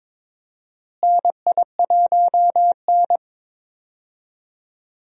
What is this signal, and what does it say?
Morse code, heard as a keyed tone, about 21 wpm, 700 Hz. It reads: NI1N
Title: USA Call Signs - 4 Characters - Repeated 3x - 22wpm